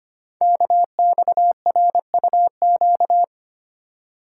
Morse code, heard as a keyed tone, about 25 wpm, 700 Hz. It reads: KXRUQ
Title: Code Groups - 5 Letters - 25wpm